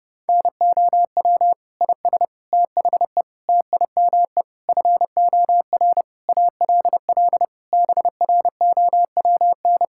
Morse code, heard as a keyed tone, about 30 wpm, 700 Hz. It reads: NOW IS THE TIME FOR ALL BROWN
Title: Pangram Challenge v2 - 30wpm